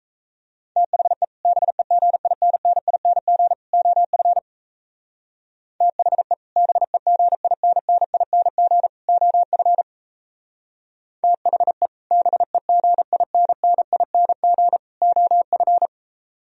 Morse code, heard as a keyed tone, about 37 wpm, 700 Hz. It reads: THE BEGINNING OF THE BEGINNING OF THE BEGINNING OF